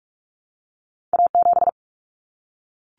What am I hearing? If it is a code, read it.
Morse code, heard as a keyed tone, about 46 wpm, 700 Hz. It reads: A7
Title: Common Cut Numbers - 2-Digit 45wpm